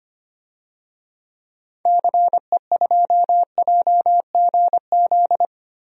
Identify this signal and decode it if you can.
Morse code, heard as a keyed tone, about 25 wpm, 700 Hz. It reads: CE2JGZ
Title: DXCC-100 Call Signs - 6 Characters - 25wpm